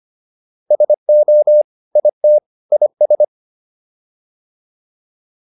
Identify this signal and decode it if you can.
Morse code, heard as a keyed tone, about 25 wpm, 600 Hz. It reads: SO IT IS